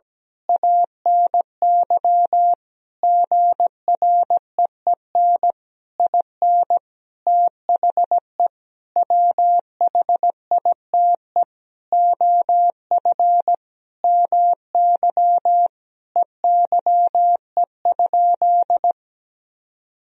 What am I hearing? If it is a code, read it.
Morse code, heard as a keyed tone, about 17 wpm, 700 Hz. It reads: ANY GREEN IN THE WHITE OF MY EYE?
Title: Sentences from Top 500 Words and Abbreviations 17wpm